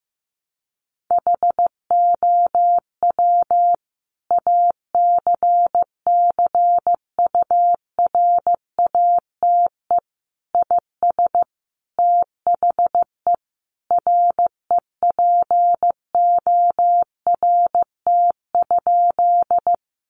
Morse code, heard as a keyed tone, about 15 wpm, 700 Hz. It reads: HOW ACCURATE IS THE REPORT?
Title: Sentences from Top 1200 Words 15wpm